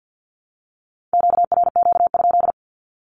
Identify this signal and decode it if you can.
Morse code, heard as a keyed tone, about 50 wpm, 700 Hz. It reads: QRQ?
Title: Q-Codes 50wpm